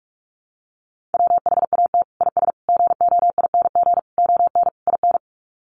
Morse code, heard as a keyed tone, about 44 wpm, 700 Hz. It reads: WHAT IS GOING ON IN
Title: Sets of 5 Words Encore2 45wpm